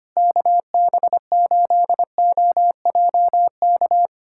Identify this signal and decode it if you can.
Morse code, heard as a keyed tone, about 25 wpm, 700 Hz. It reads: KB8OJK
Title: USA Callsign DE Callsign - 6 Characters - 25wpm